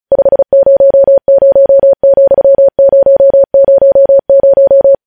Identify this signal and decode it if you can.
Morse code, heard as a keyed tone, about 36 wpm, 550 Hz. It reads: 500,000
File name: podcast / News headlines-1x-35x35